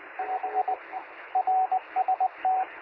Morse code, heard as a keyed tone, about 20 wpm, 750 Hz, with radio noise. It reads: 5 RST
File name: sm7lcb / 1819z sp2wpy 23cm